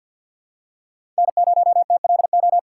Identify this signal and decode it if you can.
Morse code, heard as a keyed tone, about 53 wpm, 700 Hz. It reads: N0TLO
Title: USA Call Signs - 5 Characters - 2x Character Spacing - 50wpm